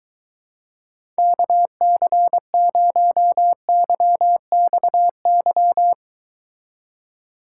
Morse code, heard as a keyed tone, about 22 wpm, 700 Hz. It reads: KC0YXY